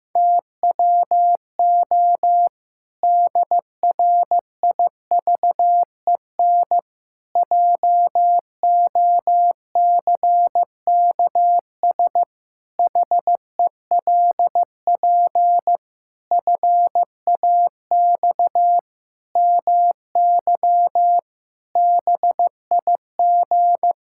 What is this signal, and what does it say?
Morse code, heard as a keyed tone, about 15 wpm, 700 Hz. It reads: TWO DRIVEN JOCKS HELP FAX MY BIG
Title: Pangram Challenge v2 - 15wpm